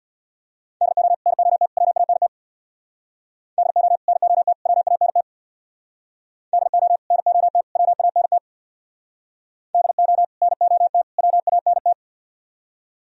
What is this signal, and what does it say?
Morse code, heard as a keyed tone, about 65 wpm, 700 Hz. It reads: DO NOT WANT DO NOT WANT DO NOT WANT DO NOT WANT